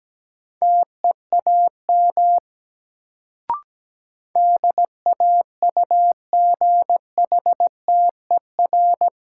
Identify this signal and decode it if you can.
Morse code, heard as a keyed tone, about 17 wpm, 700 Hz. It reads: TEAM DAUGHTER
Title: Sentences from Top 1500 Words - New Word Drill Alt - 17wpm